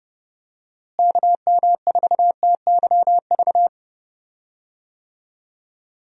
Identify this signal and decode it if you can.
Morse code, heard as a keyed tone, about 32 wpm, 700 Hz. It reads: KM4TYV
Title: USA Call Signs - 6 Characters - Repeated 3x - 30wpm